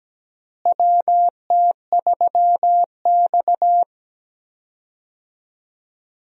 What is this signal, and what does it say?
Morse code, heard as a keyed tone, about 17 wpm, 700 Hz. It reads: WT3X
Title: USA Call Signs - 4 Characters - Repeated 2x - 17wpm